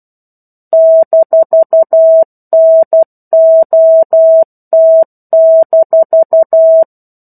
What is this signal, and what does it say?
Morse code, heard as a keyed tone, about 12 wpm, 650 Hz. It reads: -NOT-
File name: 01-Chapter-01 12 WPM